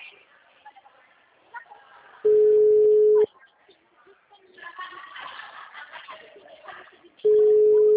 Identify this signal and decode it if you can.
Ringback tone (Europe, 425 Hz), with speech.
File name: out- -2019- - - .5845